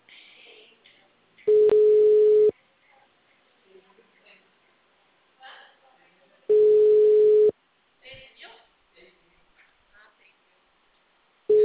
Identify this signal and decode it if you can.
Ringback tone (Europe, 425 Hz), with speech.